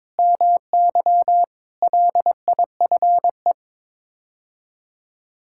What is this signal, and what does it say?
Morse code, heard as a keyed tone, about 22 wpm, 700 Hz. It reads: MY LIFE
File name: practice / 22wpm / Sets of 7 Words - 22wpm